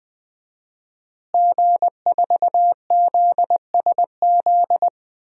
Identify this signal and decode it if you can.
Morse code, heard as a keyed tone, about 20 wpm, 700 Hz. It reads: G4ZSZ
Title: DXCC-100 Call Signs - 5 Characters - Repeated 3x - 20wpm